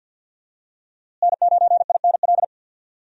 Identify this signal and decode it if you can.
Morse code, heard as a keyed tone, about 51 wpm, 700 Hz. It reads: N9INL
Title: USA Call Signs - 5 Characters - Repeated 3x - 50wpm